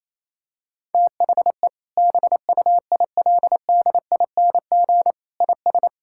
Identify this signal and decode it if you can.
Morse code, heard as a keyed tone, about 27 wpm, 700 Hz. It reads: THE BUILDING IS
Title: Sentences from Top 1000 Words - Review Part 5 - 28wpm